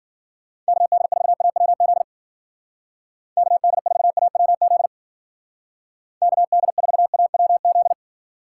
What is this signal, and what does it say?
Morse code, heard as a keyed tone, about 49 wpm, 700 Hz. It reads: KD4AWZ KD4AWZ KD4AWZ
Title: Speed Racing - USA 2x3 Call Signs 40wpm